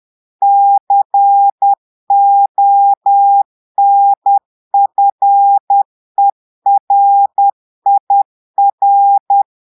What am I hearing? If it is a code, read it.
Morse code, heard as a keyed tone, about 10 wpm, 800 Hz. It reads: CONFERIR
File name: portugais texte en clair 2 vitesse 10 mots minutes